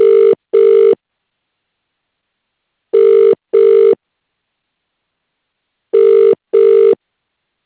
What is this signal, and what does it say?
Ringback tone (UK), 400 + 450 Hz.